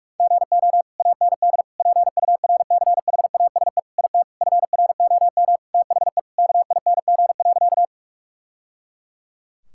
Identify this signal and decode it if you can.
Morse code, heard as a keyed tone, about 41 wpm, 700 Hz. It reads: GO AND PURCHASE IT FROM THE KING.
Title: Sentences from Top 1000 Words - Review Part 5 - 45wpm